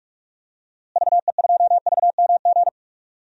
Morse code, heard as a keyed tone, about 46 wpm, 700 Hz. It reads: VE2VMG
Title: DXCC-100 Call Signs - 6 Characters - Repeated 2x - 45wpm